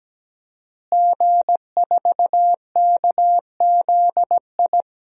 Morse code, heard as a keyed tone, about 17 wpm, 700 Hz. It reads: G4KZI
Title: DXCC-100 Call Signs - 5 Characters - Repeated 2x - 17wpm